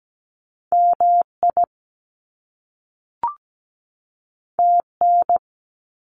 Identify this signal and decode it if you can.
Morse code, heard as a keyed tone, about 17 wpm, 700 Hz. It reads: MI TN
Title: US States 17wpm